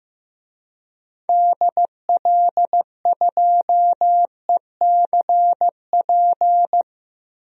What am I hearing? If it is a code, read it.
Morse code, heard as a keyed tone, about 15 wpm, 700 Hz. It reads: DL2ECP
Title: DXCC-100 Call Signs - 6 Characters - Repeated 3x - 15wpm